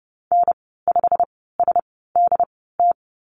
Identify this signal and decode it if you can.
Morse code, heard as a keyed tone, about 30 wpm, spaced as for 17 wpm, 700 Hz. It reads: N5SDT
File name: Lesson 19g - Letter w - Call Signs 30wpm